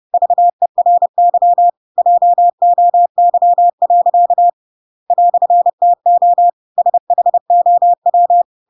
Morse code, heard as a keyed tone, about 30 wpm, 700 Hz. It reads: VERY JOY. "TO SHOW
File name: Pinocchio / Pinocchio ch8 30 30 ik1hge